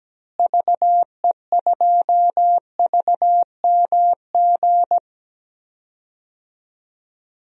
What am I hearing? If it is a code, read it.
Morse code, heard as a keyed tone, about 17 wpm, 700 Hz. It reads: VE2VMG
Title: DXCC-100 Call Signs - 6 Characters - Repeated 2x - 17wpm